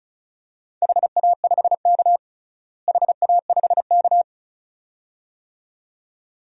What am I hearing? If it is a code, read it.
Morse code, heard as a keyed tone, about 34 wpm, 700 Hz. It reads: HA5K HA5K
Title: DXCC-100 Call Signs - 4 Characters - Repeated 2x - 35wpm